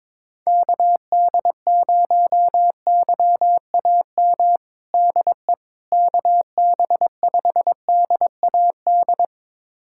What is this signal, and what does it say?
Morse code, heard as a keyed tone, about 22 wpm, 700 Hz. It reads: KD0YAM DE KB5DAD
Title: USA Callsign DE Callsign - 6 Characters - 22wpm